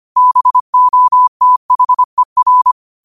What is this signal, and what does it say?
Morse code, heard as a keyed tone, about 25 wpm, 1000 Hz. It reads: BOTHER